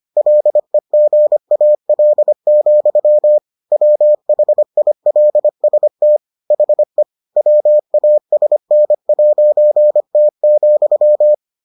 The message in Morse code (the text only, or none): LEGAL, WHILST HE WASN'T,